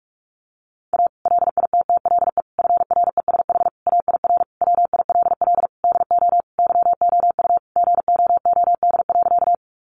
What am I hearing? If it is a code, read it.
Morse code, heard as a keyed tone, about 44 wpm, 700 Hz. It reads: A LITTLE FRESH AIR WILL DO YOU GOOD.
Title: Sentences from Top 1200 Words 45wpm